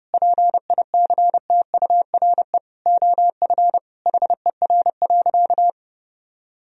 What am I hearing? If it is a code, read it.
Morse code, heard as a keyed tone, about 30 wpm, 700 Hz. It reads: PICTURE OF HER.
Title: Sentences from Top 1000 Words - Review Part 1 - 30wpm